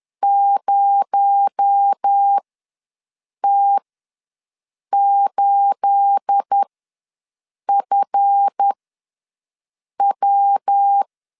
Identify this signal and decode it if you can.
Morse code, heard as a keyed tone, about 10 wpm, spaced as for 6 wpm, 800 Hz. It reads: 0T8FW
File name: LEZIONI / 10 SIGNS / 10A SLASH ws 10wpm 4121